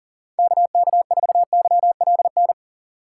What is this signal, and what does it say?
Morse code, heard as a keyed tone, about 38 wpm, 700 Hz. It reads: KK4YLN
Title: USA Call Signs - 6 Characters - 6x Character Spacing - 40wpm